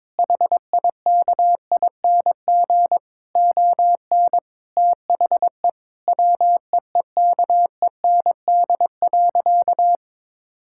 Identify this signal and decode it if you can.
Morse code, heard as a keyed tone, about 22 wpm, 700 Hz. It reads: HIKING ON THE WEEKEND.
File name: Difficult Sentences with Hints - 22wpm